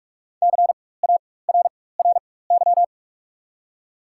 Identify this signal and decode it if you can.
Morse code, heard as a keyed tone, about 47 wpm, spaced as for 21 wpm, 700 Hz. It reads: CARRY
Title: 4x Character Spacing - Top 300 Words - 45wpm